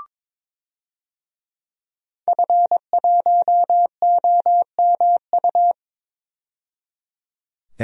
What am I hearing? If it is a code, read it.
Morse code, heard as a keyed tone, about 21 wpm, 700 Hz. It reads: F1OMU
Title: DXCC-100 Call Signs - 5 Characters - 22wpm